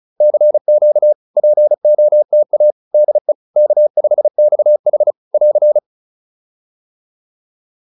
Morse code, heard as a keyed tone, about 35 wpm, 600 Hz. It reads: CQ POTA DE K5XH <AR>